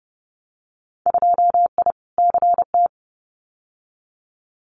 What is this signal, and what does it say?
Morse code, heard as a keyed tone, about 30 wpm, 700 Hz. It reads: 2I CT
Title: Winter Field Day Exchanges 30wpm